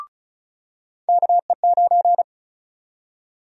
Morse code, heard as a keyed tone, about 36 wpm, 700 Hz. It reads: KE9
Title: USA Call Sign Prefixes - 35wpm